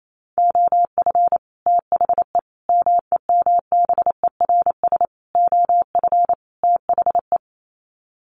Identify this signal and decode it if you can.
Morse code, heard as a keyed tone, about 28 wpm, 700 Hz. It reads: OF THE MEMBERS OF THE
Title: Sets of 5 Words Encore2 28wpm